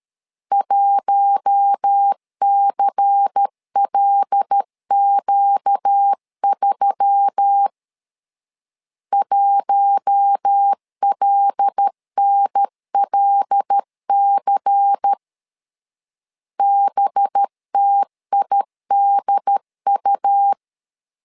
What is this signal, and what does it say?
Morse code, heard as a keyed tone, about 13 wpm, 800 Hz. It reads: 1CLQ3 1LNLC BTIDU
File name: LEZIONI / 5 LBVC / 5 LBVC 12wpm 1955